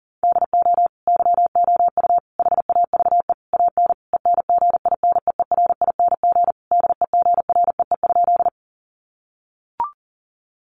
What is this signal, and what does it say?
Morse code, heard as a keyed tone, about 38 wpm, 700 Hz. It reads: DO YOU HAVE AN ENGINEERING DEGREE?
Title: Sentences from Top 1500 Words 40wpm